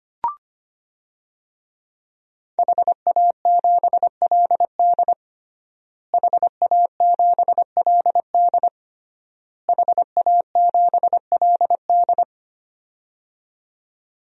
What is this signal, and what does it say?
Morse code, heard as a keyed tone, about 25 wpm, 700 Hz. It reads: HA7LD HA7LD HA7LD
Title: DXCC-100 Call Signs - 5 Characters - Repeated 3x - 25wpm